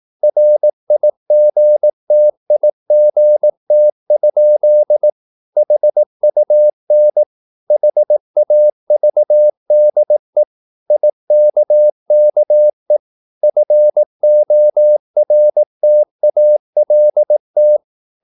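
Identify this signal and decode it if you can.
Morse code, heard as a keyed tone, about 18 wpm, 600 Hz. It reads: RIGTIGT? HUN HAVDE IKKE FORTALT
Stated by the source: Slow (QRS) Morse practice at 18 wpm, CW med Gnister.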